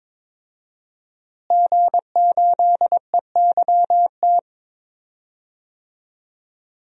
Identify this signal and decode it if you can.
Morse code, heard as a keyed tone, about 22 wpm, 700 Hz. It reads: G8EYT
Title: DXCC-100 Call Signs - 5 Characters - Repeated 3x - 22wpm